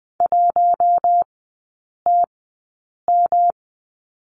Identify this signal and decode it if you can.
Morse code, heard as a keyed tone, about 20 wpm, spaced as for 8 wpm, 700 Hz. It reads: 1TM
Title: 198 Lesson 32e - Letter k - Three Characters 20wpm